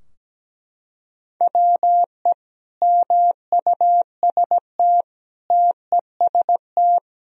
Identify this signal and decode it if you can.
Morse code, heard as a keyed tone, about 17 wpm, 700 Hz. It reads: WE MUST TEST
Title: Sentences from Top 1000 Words - Review Part 3 - 17wpm